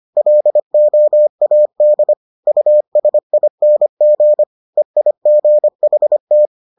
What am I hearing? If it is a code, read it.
Morse code, heard as a keyed tone, about 25 wpm, 600 Hz. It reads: LOAD USING EIGHT